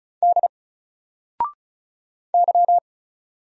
Morse code, heard as a keyed tone, about 36 wpm, 700 Hz. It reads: D Y
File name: Single-Letter-Number-35wpm